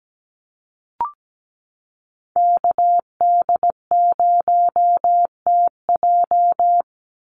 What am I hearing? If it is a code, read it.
Morse code, heard as a keyed tone, about 17 wpm, 700 Hz. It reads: KD0TJ
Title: USA Call Signs 17wpm